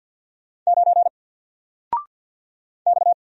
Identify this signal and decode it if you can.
Morse code, heard as a keyed tone, about 48 wpm, 700 Hz. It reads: 9 X
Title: Single Character (Letter:Number:Punctuation:Prosign) Alt 50wpm